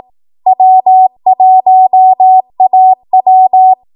Morse code, heard as a keyed tone, about 18 wpm, 750 Hz. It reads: W1AW
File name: Archive / 18 WPM / 18WPM